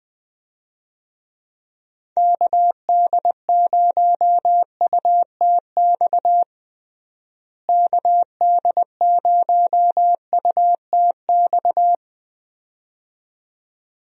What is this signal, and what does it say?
Morse code, heard as a keyed tone, about 20 wpm, 700 Hz. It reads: KD0UTX KD0UTX
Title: USA Call Signs - 6 Characters - Repeated 2x - 20wpm